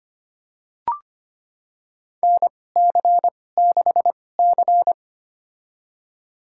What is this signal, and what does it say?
Morse code, heard as a keyed tone, about 25 wpm, spaced as for 18 wpm, 700 Hz. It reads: NC6C
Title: USA Call Signs - 4 Characters - 2x Character Spacing - 25wpm